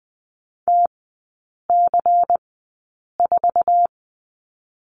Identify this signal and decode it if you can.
Morse code, heard as a keyed tone, about 21 wpm, spaced as for 8 wpm, 700 Hz. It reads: TC4
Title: 145 Lesson 24e - Letter y - Three Characters 20wpm